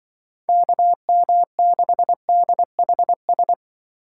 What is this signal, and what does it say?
Morse code, heard as a keyed tone, about 24 wpm, 700 Hz. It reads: KM6DHS
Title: Speed Racing - USA 2x3 Call Signs 20wpm